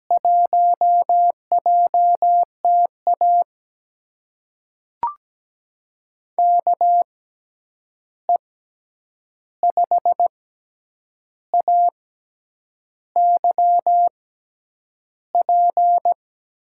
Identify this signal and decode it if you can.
Morse code, heard as a keyed tone, about 17 wpm, spaced as for 8 wpm, 700 Hz. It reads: 1JTA KE5AYP
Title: USA Call Signs - 6x Character Spacing - 17wpm